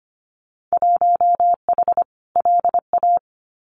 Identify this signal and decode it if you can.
Morse code, heard as a keyed tone, about 25 wpm, 700 Hz. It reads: 1H LA
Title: Winter Field Day Exchanges 25wpm